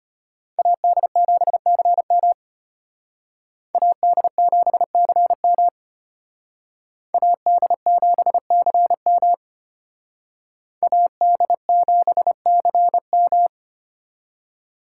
Morse code, heard as a keyed tone, about 32 wpm, 700 Hz. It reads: AD7CM AD7CM AD7CM AD7CM